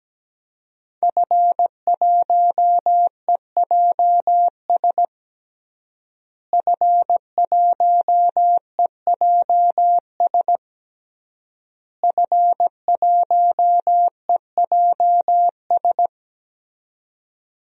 Morse code, heard as a keyed tone, about 17 wpm, 700 Hz. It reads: F1EJS F1EJS F1EJS
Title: DXCC-100 Call Signs - 5 Characters - Repeated 3x - 17wpm